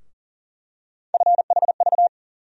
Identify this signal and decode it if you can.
Morse code, heard as a keyed tone, about 40 wpm, 700 Hz. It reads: FHV
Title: USA Call Sign Postfixes - 3 Letters 40wpm